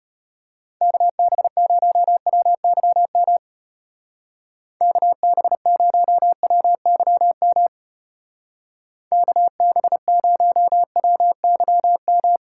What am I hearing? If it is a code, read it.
Morse code, heard as a keyed tone, about 30 wpm, 700 Hz. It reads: KB0WYM KB0WYM KB0WYM
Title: Speed Racing - USA 2x3 Call Signs 25wpm